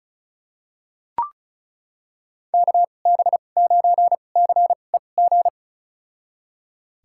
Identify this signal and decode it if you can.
Morse code, heard as a keyed tone, about 36 wpm, spaced as for 25 wpm, 690 Hz. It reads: KB9CEG
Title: USA Call Signs - 6 Characters - 2x Character Spacing - 35wpm